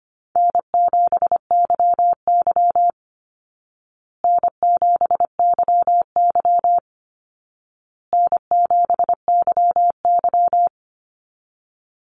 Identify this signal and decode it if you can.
Morse code, heard as a keyed tone, about 25 wpm, 700 Hz. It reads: N7YY N7YY N7YY
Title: Easy Call Signs Repeated 3x - 25wpm